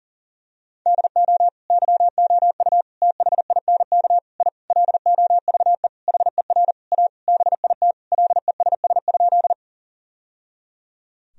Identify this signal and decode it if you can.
Morse code, heard as a keyed tone, about 37 wpm, 700 Hz. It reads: DO YOU THINK I LOVE HER A BIT LESS?
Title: Sentences from Top 1000 Words - Review Part 2 - 40wpm